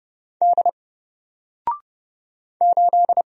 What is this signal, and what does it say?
Morse code, heard as a keyed tone, about 30 wpm, 700 Hz. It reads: D 8
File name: Single-Letter-Number-30wpm